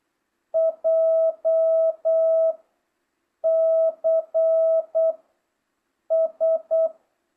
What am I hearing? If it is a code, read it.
Morse code, heard as a keyed tone, about 8 wpm, 650 Hz. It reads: J C S